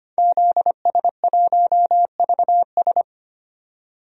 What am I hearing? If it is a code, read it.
Morse code, heard as a keyed tone, about 25 wpm, 700 Hz. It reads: ZS1VS
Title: DXCC-100 Call Signs - 5 Characters - 25wpm